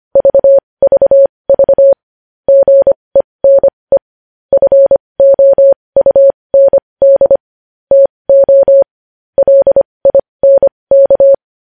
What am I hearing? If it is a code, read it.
Morse code, heard as a keyed tone, about 25 wpm, spaced as for 20 wpm, 550 Hz. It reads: VVV GENE FOUND TO LINK
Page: news-headlines-in-morse-code-at-20-wpm